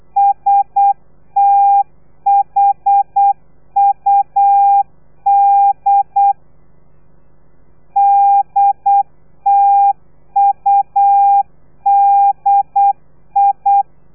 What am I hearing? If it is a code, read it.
Morse code, heard as a keyed tone, about 8 wpm, 800 Hz. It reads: STHUD DTUDI
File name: telegrafia / i2tfj / Lez3 35